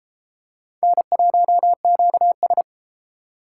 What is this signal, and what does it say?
Morse code, heard as a keyed tone, about 32 wpm, 700 Hz. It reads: N1QS